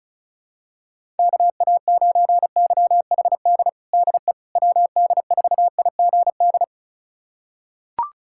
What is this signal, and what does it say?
Morse code, heard as a keyed tone, about 36 wpm, 700 Hz. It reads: KA9YHD DE WD4IGD
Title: USA Callsign DE Callsign 35wpm